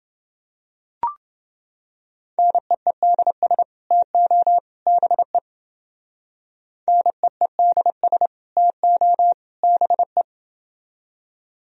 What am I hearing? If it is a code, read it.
Morse code, heard as a keyed tone, about 29 wpm, 700 Hz. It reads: NEEDS TO BE NEEDS TO BE